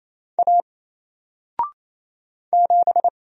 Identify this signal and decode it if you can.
Morse code, heard as a keyed tone, about 27 wpm, 700 Hz. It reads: A 7
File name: Single-Letter-Number-28wpm